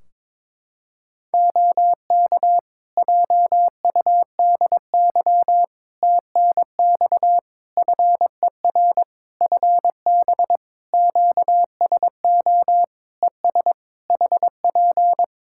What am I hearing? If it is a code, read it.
Morse code, heard as a keyed tone, about 22 wpm, 700 Hz. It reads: OK JUDY TNX FER FB QSO ES HP
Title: Practice QSO - Exchange 4 - Ending - 22wpm